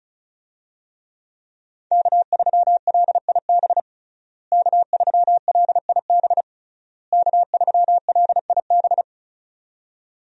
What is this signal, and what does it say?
Morse code, heard as a keyed tone, about 34 wpm, 700 Hz. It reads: K3LIB K3LIB K3LIB
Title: USA Call Signs - 5 Characters - Repeated 3x - 35wpm